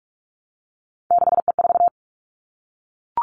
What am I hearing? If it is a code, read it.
Morse code, heard as a keyed tone, about 46 wpm, 700 Hz. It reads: 6E4
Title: Three Challenging Letter-Number Combinations 45wpm